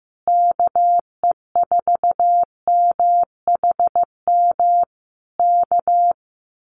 Morse code, heard as a keyed tone, about 15 wpm, 700 Hz. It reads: KE4MHM K
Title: Common QSO Phrases Encore 15wpm